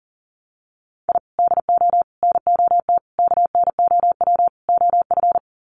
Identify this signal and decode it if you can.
Morse code, heard as a keyed tone, about 40 wpm, 700 Hz. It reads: I DO NOT KNOW OF
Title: Sets of 5 Words Encore2 40wpm